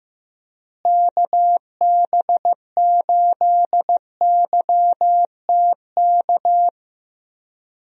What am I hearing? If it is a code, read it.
Morse code, heard as a keyed tone, about 15 wpm, 700 Hz. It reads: KB8YTK
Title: USA Call Signs - 6 Characters - Repeated 2x - 15wpm